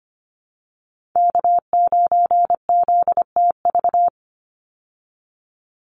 Morse code, heard as a keyed tone, about 25 wpm, 700 Hz. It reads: K9ZTV
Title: Call Signs Repeated 2x - 25wpm